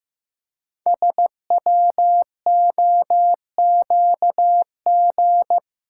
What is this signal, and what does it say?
Morse code, heard as a keyed tone, about 15 wpm, 700 Hz. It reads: SWOQG
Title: Code Groups - 5 Letters - 15wpm